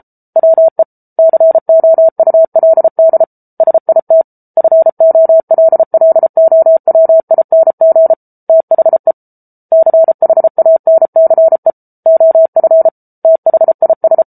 Morse code, heard as a keyed tone, about 34 wpm, 650 Hz. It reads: WE COULD SIT FOLLOWING THE CHANCE OF THIS